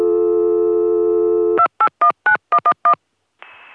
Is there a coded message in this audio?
Yes: DTMF from 1.6 to 2.9 s, heard as telephone touch tones, tones of about 80 ms. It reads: 2819111